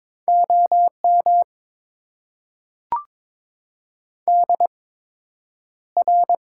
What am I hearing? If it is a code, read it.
Morse code, heard as a keyed tone, about 22 wpm, spaced as for 9 wpm, 700 Hz. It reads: OM DR
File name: Top 90 QSO Elements - 8x Character Spacing - 22wpm